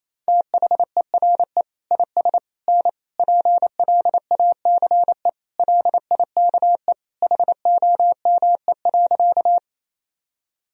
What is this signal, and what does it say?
Morse code, heard as a keyed tone, about 28 wpm, 700 Hz. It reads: THERE IS N PLACE LIKE HOME.
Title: Sentences from Top 200 Words and Abbreviations 28wpm